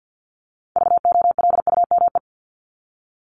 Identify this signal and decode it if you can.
Morse code, heard as a keyed tone, about 50 wpm, 700 Hz. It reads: VOLUME
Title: Top 1751-2000 Words 50wpm